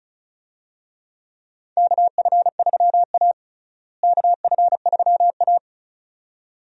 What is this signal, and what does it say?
Morse code, heard as a keyed tone, about 34 wpm, 700 Hz. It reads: KF3A KF3A
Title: USA Call Signs - 4 Characters - Repeated 2x - 35wpm